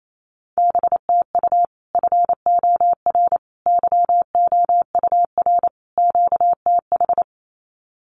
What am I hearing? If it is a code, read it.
Morse code, heard as a keyed tone, about 28 wpm, 700 Hz. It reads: BTU FOR YOUR QTH
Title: Common QSO Phrases 28wpm